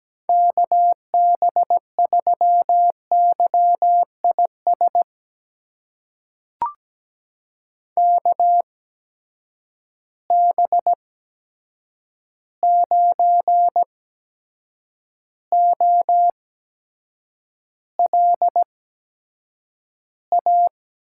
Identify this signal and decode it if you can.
Morse code, heard as a keyed tone, about 17 wpm, spaced as for 7 wpm, 700 Hz. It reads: KB3YIS KB9OLA
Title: USA Call Signs - 8x Character Spacing - 17wpm